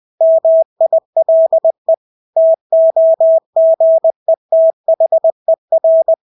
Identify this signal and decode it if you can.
Morse code, heard as a keyed tone, about 20 wpm, 650 Hz. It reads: MILE TOGETHER
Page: Engelske ord 20 wpm